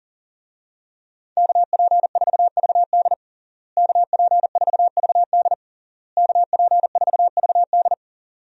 Morse code, heard as a keyed tone, about 40 wpm, 700 Hz. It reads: KP4VD KP4VD KP4VD
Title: DXCC-100 Call Signs - 5 Characters - Repeated 3x - 40wpm